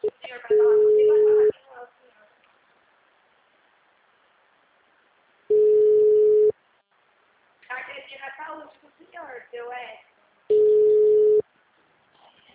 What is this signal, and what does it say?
Ringback tone (Europe, 425 Hz), with speech.